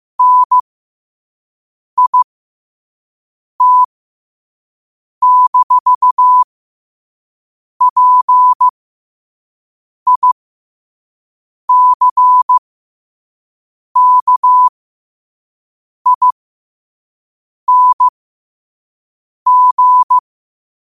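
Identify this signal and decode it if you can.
Morse code, heard as a keyed tone, about 15 wpm, spaced as for 5 wpm, 1000 Hz. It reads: NIT-PICKING
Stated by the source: Morse code sent at 5 words per minute.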